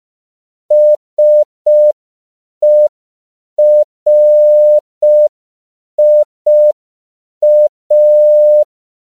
Morse code, heard as a keyed tone, about 5 wpm, 600 Hz. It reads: SERIA
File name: CW / 5PPM